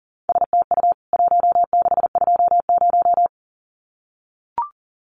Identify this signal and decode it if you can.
Morse code, heard as a keyed tone, about 40 wpm, 700 Hz. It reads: STU 1620
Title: CWT Exchange 40wpm